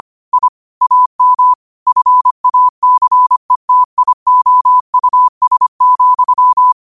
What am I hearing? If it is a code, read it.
Morse code, heard as a keyed tone, about 25 wpm, 1000 Hz. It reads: I AM FACETIOUS,
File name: qotdfiles / 25wpm